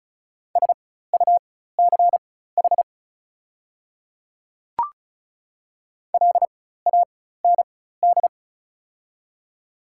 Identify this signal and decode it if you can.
Morse code, heard as a keyed tone, about 36 wpm, spaced as for 16 wpm, 700 Hz. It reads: SUCH LAND
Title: 4x Character Spacing - Top 200 Words - 35wpm